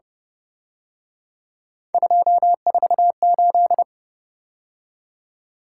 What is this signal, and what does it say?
Morse code, heard as a keyed tone, about 31 wpm, 700 Hz. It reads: 248
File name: Speed-Racing - Three Letter-Number Combinations - 25wpm